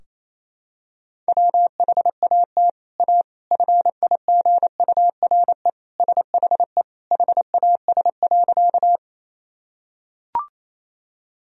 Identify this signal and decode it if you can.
Morse code, heard as a keyed tone, about 27 wpm, 700 Hz. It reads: WHAT A FIGURE SHE HAS.
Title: Sentences from Top 1000 Words - Review Part 2 - 28wpm